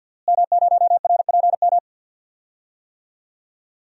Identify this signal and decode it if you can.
Morse code, heard as a keyed tone, about 46 wpm, 700 Hz. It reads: M0RPM
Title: DXCC-100 Call Signs - 5 Characters - Repeated 3x - 50wpm